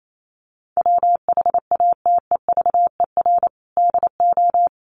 Morse code, heard as a keyed tone, about 28 wpm, 700 Hz. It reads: WHATEVER DO
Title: Sentences from Top 1100 Words 28wpm